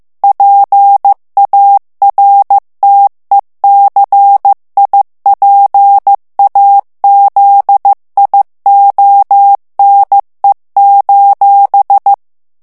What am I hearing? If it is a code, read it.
Morse code, heard as a keyed tone, about 15 wpm, 800 Hz. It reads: PARTECIPAZIONE:
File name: morse mp3 / ITALIEN texte clair 2 vitesse 14 mots minute